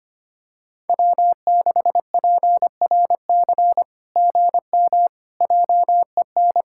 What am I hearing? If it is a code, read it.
Morse code, heard as a keyed tone, about 24 wpm, 700 Hz. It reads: W6PRC GM JEN
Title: SOTA QSO Practice 25wpm